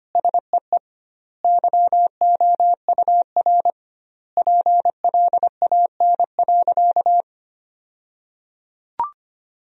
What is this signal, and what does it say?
Morse code, heard as a keyed tone, about 25 wpm, 700 Hz. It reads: SEE YOUR PLAN.
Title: 2x Word-Spacing - Sentences from Top 500 Words - 25wpm